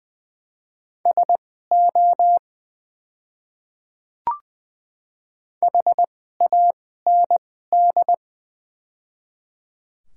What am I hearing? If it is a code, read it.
Morse code, heard as a keyed tone, about 20 wpm, spaced as for 14 wpm, 700 Hz. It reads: SO HAND
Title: 2x Character Spacing - Top 200 Words - 20wpm